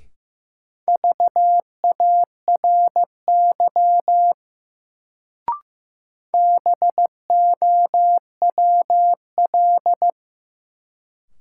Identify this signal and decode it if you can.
Morse code, heard as a keyed tone, about 15 wpm, 700 Hz. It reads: VARY BOWL
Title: Sentences from Top 1200 Words - New Word Drill Alt - 15wpm